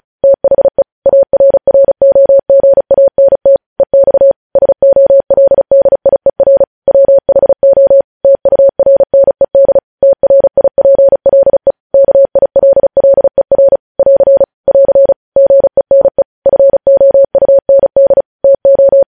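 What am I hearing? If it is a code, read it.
Morse code, heard as a keyed tone, about 36 wpm, 550 Hz. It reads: THE ARROGANT EX SOLDIER WHO TURNED TRIPLE KILLER <AR> <AR> GENE FOUND TO
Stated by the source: News headlines sent in Morse code at 35 WPM.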